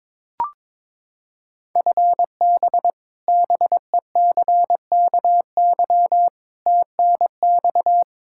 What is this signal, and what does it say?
Morse code, heard as a keyed tone, about 22 wpm, 700 Hz. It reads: FB BECKY TNX
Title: Practice QSO - Exchange 3 - More Info - 22wpm